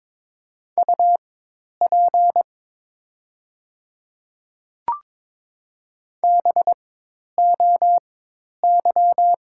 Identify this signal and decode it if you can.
Morse code, heard as a keyed tone, about 22 wpm, spaced as for 10 wpm, 700 Hz. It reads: UP BOY
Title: 4x Character Spacing - Top 200 Words - 22wpm